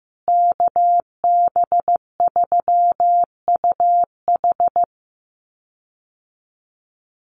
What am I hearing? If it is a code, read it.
Morse code, heard as a keyed tone, about 15 wpm, 700 Hz. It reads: KB3UH
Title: USA Call Signs Repeated 3x 15wpm